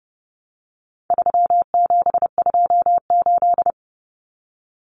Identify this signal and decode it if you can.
Morse code, heard as a keyed tone, about 30 wpm, 700 Hz. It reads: 3728
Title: Common Cut Numbers - 4-Digit 30wpm